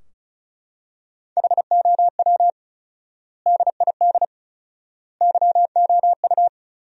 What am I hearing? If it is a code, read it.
Morse code, heard as a keyed tone, about 36 wpm, 700 Hz. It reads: HOW DID YOU
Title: 4x Word-Spacing - Sentences from Top 100 Words - 35wpm